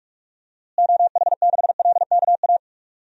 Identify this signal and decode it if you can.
Morse code, heard as a keyed tone, about 41 wpm, 700 Hz. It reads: OH6LKA
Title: DXCC-100 Call Signs - 6 Characters - Repeated 3x - 45wpm